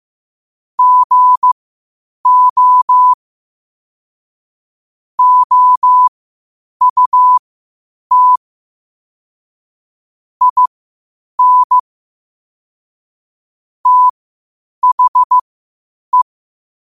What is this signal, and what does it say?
Morse code, heard as a keyed tone, about 15 wpm, spaced as for 8 wpm, 1000 Hz. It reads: GO OUT IN THE
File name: 8wpm